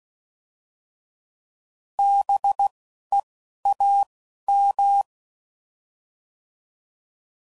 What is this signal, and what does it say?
Morse code, heard as a keyed tone, about 16 wpm, 800 Hz. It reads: B E A M